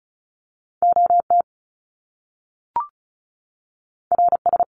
Morse code, heard as a keyed tone, about 36 wpm, 700 Hz. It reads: OT RS
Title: Top Two Letter Combinations in Words 35wpm